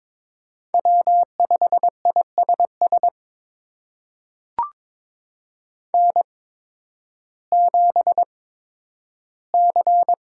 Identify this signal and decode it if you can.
Morse code, heard as a keyed tone, about 21 wpm, spaced as for 12 wpm, 700 Hz. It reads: W5ISS N7C
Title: USA Call Signs - 5 Characters - 8x Character Spacing - 22wpm